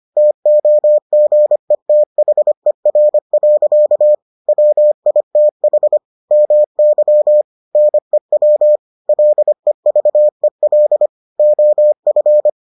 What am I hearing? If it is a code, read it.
Morse code, heard as a keyed tone, about 25 wpm, 600 Hz. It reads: TOGETHER. WITH MY NEW LEVEL OF